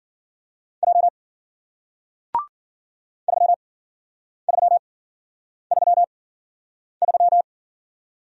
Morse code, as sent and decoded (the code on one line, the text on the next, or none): .--- / ...-- ...-- ...-- ...--
J 3333